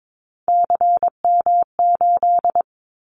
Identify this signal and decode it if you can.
Morse code, heard as a keyed tone, about 22 wpm, 700 Hz. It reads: CM8
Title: Callsign Prefixes for Top 100 DXCC - 22wpm